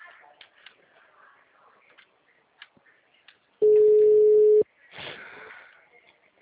Ringback tone (Europe, 425 Hz).